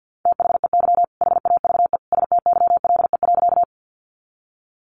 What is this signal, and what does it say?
Morse code, heard as a keyed tone, about 50 wpm, 700 Hz. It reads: THEY HAVE STYLE.
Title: Sentences from Top 800 Words 50wpm